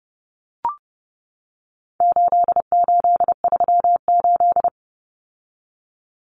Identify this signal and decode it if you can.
Morse code, heard as a keyed tone, about 30 wpm, 700 Hz. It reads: 8838
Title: Numbers - 4-Digit 30wpm